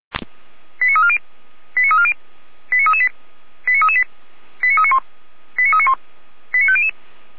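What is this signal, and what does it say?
Five-tone selective call (ZVEI), heard as quick runs of five tones, tones of about 70 ms. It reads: 89340 89340 89338 89338 89391 89391 89500
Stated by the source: ZVEI five-tone selective call received on an SDR, archived by Feuerwehr Orlamünde (fire brigade).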